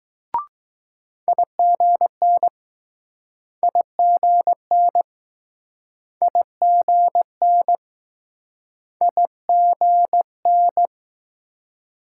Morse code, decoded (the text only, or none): IGN IGN IGN IGN